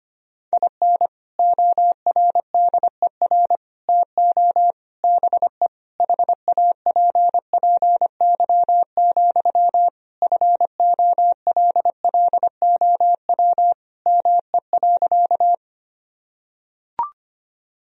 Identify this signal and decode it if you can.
Morse code, heard as a keyed tone, about 25 wpm, 700 Hz. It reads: IN ORDER TO BE HAPPY, FOLLOW ME.
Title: Sentences from Top 1000 Words - Review Part 2 - 25wpm